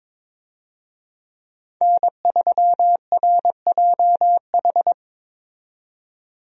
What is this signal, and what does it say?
Morse code, heard as a keyed tone, about 22 wpm, 700 Hz. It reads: N3RJH
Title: USA Call Signs - 5 Characters - 22wpm